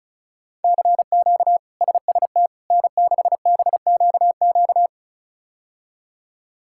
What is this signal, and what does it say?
Morse code, heard as a keyed tone, about 34 wpm, 700 Hz. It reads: CQ SST N6BQQ
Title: K1USN SST QSO Practice - 35wpm